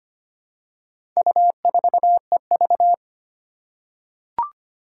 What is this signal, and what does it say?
Morse code, heard as a keyed tone, about 25 wpm, 700 Hz. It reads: U4EV
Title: Four Challenging Letter-Number Combinations 25wpm